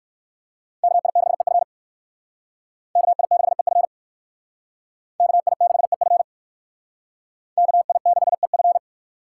Morse code, heard as a keyed tone, about 56 wpm, 700 Hz. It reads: KI6EF KI6EF KI6EF KI6EF